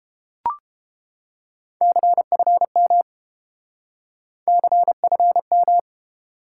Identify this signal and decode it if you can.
Morse code, heard as a keyed tone, about 30 wpm, 700 Hz. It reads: CFM CFM